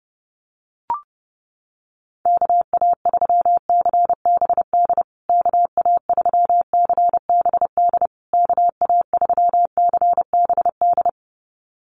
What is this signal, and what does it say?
Morse code, heard as a keyed tone, about 30 wpm, 700 Hz. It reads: KA3CBD KA3CBD KA3CBD
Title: USA Call Signs Repeated 3x 30wpm